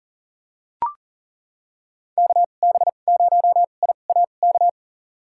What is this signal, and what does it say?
Morse code, heard as a keyed tone, about 40 wpm, spaced as for 29 wpm, 700 Hz. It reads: KB0IAK
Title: USA Call Signs - 6 Characters - 2x Character Spacing - 40wpm